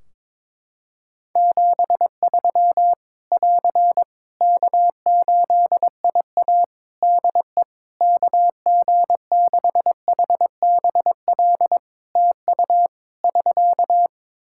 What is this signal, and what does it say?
Morse code, heard as a keyed tone, about 22 wpm, 700 Hz. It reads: 73 <AR> K8IA DE KG6HBL TU <SK>
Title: Practice QSO - Exchange 4 - Ending - 22wpm